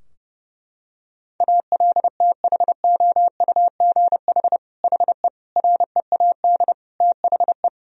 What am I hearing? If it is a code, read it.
Morse code, heard as a keyed tone, about 30 wpm, 700 Hz. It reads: ALTHOUGH HE READ THE
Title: Sentences from Top 1000 Words - Review Part 3 - 30wpm